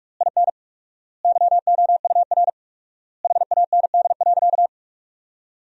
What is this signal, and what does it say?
Morse code, heard as a keyed tone, about 46 wpm, 700 Hz. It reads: IN YOUR HAND.
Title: 4x Word-Spacing - Sentences from Top 400 Words - 45wpm